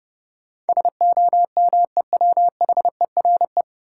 Morse code, heard as a keyed tone, about 30 wpm, 700 Hz. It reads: SOMEWHERE
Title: Sentences from Top 1400 Words - New Word Drill Alt - 30wpm